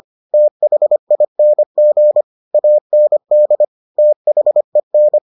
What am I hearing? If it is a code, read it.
Morse code, heard as a keyed tone, about 25 wpm, 600 Hz. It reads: THING AND THEN